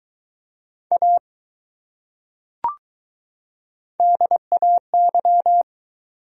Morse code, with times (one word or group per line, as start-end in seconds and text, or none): 0.9-1.2 A
4.0-5.6 DAY